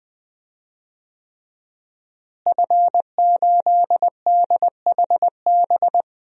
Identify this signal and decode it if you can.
Morse code, heard as a keyed tone, about 21 wpm, 700 Hz. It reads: F8DHB
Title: DXCC-100 Call Signs - 5 Characters - Repeated 3x - 20wpm